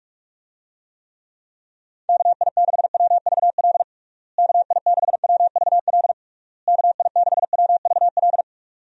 Morse code, heard as a keyed tone, about 47 wpm, 700 Hz. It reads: KI6WVL KI6WVL KI6WVL
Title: USA Call Signs - 6 Characters - Repeated 3x - 45wpm